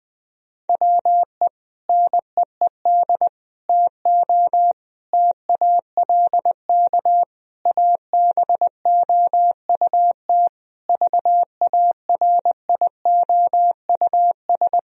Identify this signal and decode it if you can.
Morse code, heard as a keyed tone, about 20 wpm, 700 Hz. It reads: WE NEED TO TALK ABOUT VARIOUS
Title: Sentences from Top 1000 Words - Review Part 4 - 20wpm